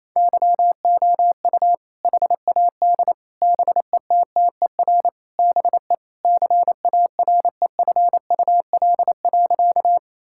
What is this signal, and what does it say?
Morse code, heard as a keyed tone, about 28 wpm, 700 Hz. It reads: YOU HAD BETTER BE CAREFUL.
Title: Sentences from Top 1600 Words 28wpm